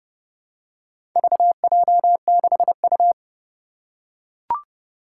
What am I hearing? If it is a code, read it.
Morse code, heard as a keyed tone, about 30 wpm, 700 Hz. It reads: VJ6U
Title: Four Challenging Letter-Number Combinations 30wpm